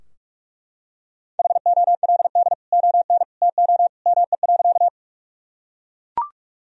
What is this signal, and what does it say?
Morse code, heard as a keyed tone, about 44 wpm, 700 Hz. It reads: HOLD ON TO ME.
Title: Sentences from Top 1000 Words - Review Part 2 - 45wpm